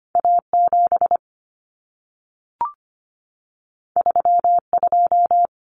Morse code, heard as a keyed tone, about 25 wpm, 700 Hz. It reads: A7 32
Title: Common Cut Numbers - 2-Digit 25wpm